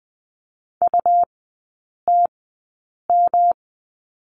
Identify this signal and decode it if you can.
Morse code, heard as a keyed tone, about 20 wpm, spaced as for 8 wpm, 700 Hz. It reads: UTM
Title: 172 Lesson 28e - Number 9 - Three Characters 20wpm